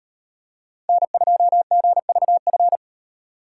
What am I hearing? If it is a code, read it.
Morse code, heard as a keyed tone, about 38 wpm, 700 Hz. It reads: N2GVF